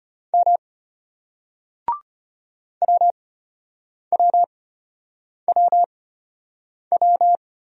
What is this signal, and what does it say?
Morse code, heard as a keyed tone, about 33 wpm, spaced as for 8 wpm, 700 Hz. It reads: M WWWW